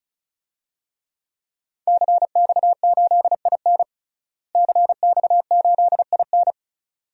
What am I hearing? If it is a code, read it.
Morse code, heard as a keyed tone, about 36 wpm, 700 Hz. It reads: CX8IN CX8IN
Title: DXCC-100 Call Signs - 5 Characters - Repeated 2x - 35wpm